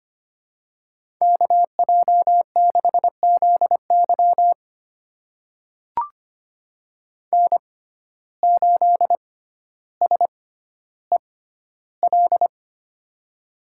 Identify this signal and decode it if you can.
Morse code, heard as a keyed tone, about 25 wpm, spaced as for 13 wpm, 700 Hz. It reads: KJ6ZY N8SEL
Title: USA Call Signs - 5 Characters - 6x Character Spacing - 25wpm